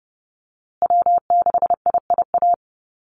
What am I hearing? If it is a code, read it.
Morse code, heard as a keyed tone, about 30 wpm, 700 Hz. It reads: W6IIA
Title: USA Call Signs Repeated 2x 30wpm